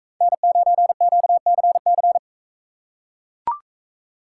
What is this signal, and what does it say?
Morse code, heard as a keyed tone, about 42 wpm, 700 Hz. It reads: N9QCC